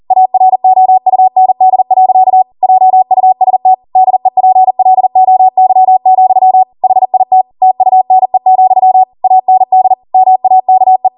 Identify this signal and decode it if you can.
Morse code, heard as a keyed tone, about 40 wpm, 750 Hz. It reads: AROUND. JUST DEPLOY, HIT TUNE, AND MAKE